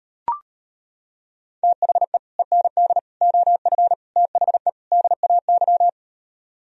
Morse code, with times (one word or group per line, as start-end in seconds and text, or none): 1.6-2.2 THE
2.4-3.0 END
3.2-3.9 OF
4.2-4.7 THE
4.9-5.9 DAY